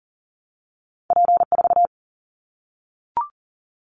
Morse code, heard as a keyed tone, about 42 wpm, 700 Hz. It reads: P4
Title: Callsign Prefixes for Top 100 DXCC - 40wpm